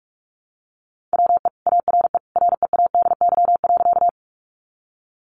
Morse code, heard as a keyed tone, about 44 wpm, 700 Hz. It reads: WE ARE READY.
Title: Sentences from Top 400 Words 45wpm